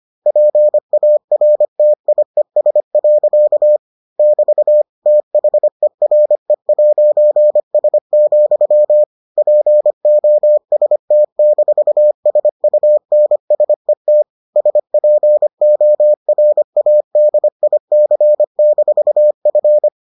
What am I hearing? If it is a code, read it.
Morse code, heard as a keyed tone, about 25 wpm, 600 Hz. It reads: PARTIES. <BT> THERE'S, POST-SUNSET SPORADIC-F